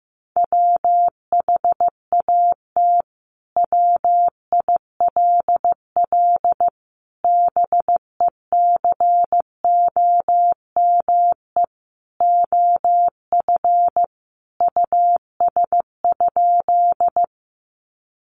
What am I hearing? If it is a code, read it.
Morse code, heard as a keyed tone, about 15 wpm, 700 Hz. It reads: WHAT WILL BECOME OF US?
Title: Sentences from Top 600 Words 15wpm